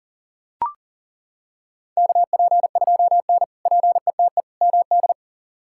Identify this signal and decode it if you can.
Morse code, heard as a keyed tone, about 37 wpm, 700 Hz. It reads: KP2N PETE MD
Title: K1USN SST QSO Practice - 40wpm